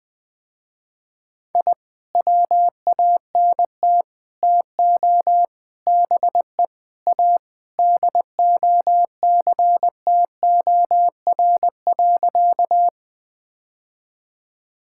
Morse code, heard as a keyed tone, about 21 wpm, 700 Hz. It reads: I WANT TO BE A DOCTOR.
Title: Sentences with 3-Word Phrases - Spoken Beginning - 20wpm